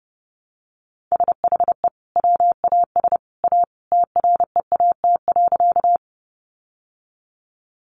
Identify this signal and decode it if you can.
Morse code, heard as a keyed tone, about 30 wpm, 700 Hz. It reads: SHE WAS A TREAT.
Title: Sentences from Top 1300 Words 30wpm